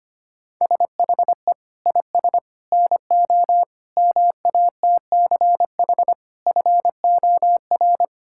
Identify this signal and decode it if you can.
Morse code, heard as a keyed tone, about 25 wpm, 700 Hz. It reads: SHE IS NO MATCH FOR
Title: Sentences from Top 1600 Words 25wpm